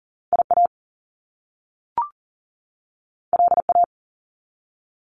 Morse code, heard as a keyed tone, about 40 wpm, 700 Hz. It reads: IA LA
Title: US States 40wpm